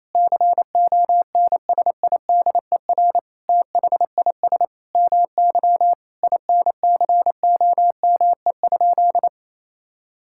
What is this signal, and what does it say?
Morse code, heard as a keyed tone, about 28 wpm, 700 Hz. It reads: CONSIDER THIS MY INCOME?
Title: Sentences from Top 1000 Words - Review Part 5 - 28wpm